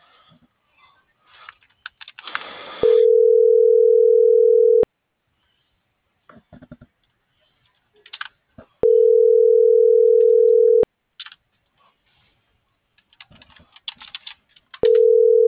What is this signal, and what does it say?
Ringback tone (North America), 440 + 480 Hz.